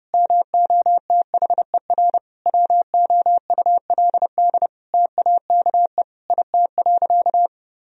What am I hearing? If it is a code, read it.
Morse code, heard as a keyed tone, about 32 wpm, 700 Hz. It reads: MOTHER WOULD TAKE IT.
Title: Sentences from Top 1000 Words - Review Part 4 - 30wpm